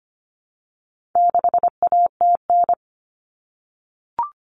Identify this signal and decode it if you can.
Morse code, heard as a keyed tone, about 25 wpm, 700 Hz. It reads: 6ATN
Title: All Cut Numbers - 4-Digit 25wpm